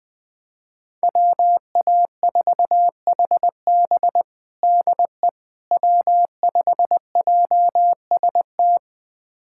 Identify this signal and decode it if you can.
Morse code, heard as a keyed tone, about 21 wpm, 700 Hz. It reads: WA4HB DE W5JST
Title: USA Callsign DE Callsign - 5 Characters - 20wpm